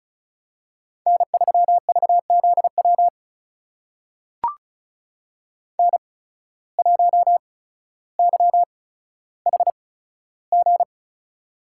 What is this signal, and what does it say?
Morse code, heard as a keyed tone, about 36 wpm, spaced as for 15 wpm, 700 Hz. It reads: N3VZW N1YHG
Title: USA Call Signs - 5 Characters - 8x Character Spacing - 35wpm